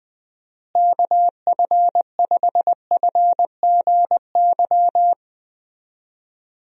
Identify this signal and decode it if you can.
Morse code, heard as a keyed tone, about 20 wpm, 700 Hz. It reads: KF5FGY
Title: USA Call Signs - 6 Characters - Repeated 2x - 20wpm